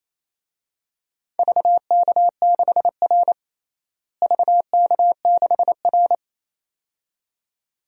Morse code, heard as a keyed tone, about 28 wpm, 700 Hz. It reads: VK6R VK6R
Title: DXCC-100 Call Signs - 4 Characters - Repeated 2x - 28wpm